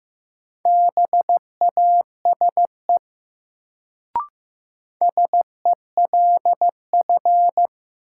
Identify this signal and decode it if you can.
Morse code, heard as a keyed tone, about 15 wpm, 700 Hz. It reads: BASE SELF
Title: Sentences from Top 1000 Words - Review Part 1 - New Word Drill - 15wpm